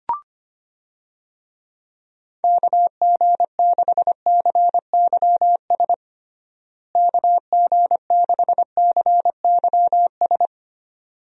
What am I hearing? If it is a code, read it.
Morse code, heard as a keyed tone, about 25 wpm, 700 Hz. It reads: KG6CYS KG6CYS
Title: USA Call Signs - 6 Characters - Repeated 2x - 25wpm